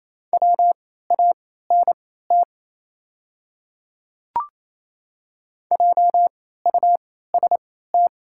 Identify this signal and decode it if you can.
Morse code, heard as a keyed tone, about 28 wpm, spaced as for 16 wpm, 700 Hz. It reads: WANT JUST
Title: 3x Character Spacing - Top 300 Words - 28wpm